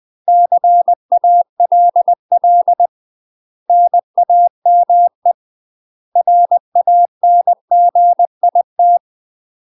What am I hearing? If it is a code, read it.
Morse code, heard as a keyed tone, about 20 wpm, 700 Hz. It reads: CALL NAME RANGIT
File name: UK MASTER QSOs TRAINING / 20WPM / UK MASTER QSOs 0039